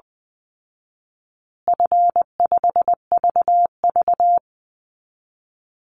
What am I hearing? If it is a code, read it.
Morse code, heard as a keyed tone, about 20 wpm, 700 Hz. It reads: F5VV
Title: Easy Call Signs Repeated 3x - 20wpm